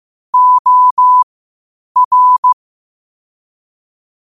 Morse code, heard as a keyed tone, about 15 wpm, spaced as for 8 wpm, 1000 Hz. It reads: OR
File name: qotdfiles / 8wpm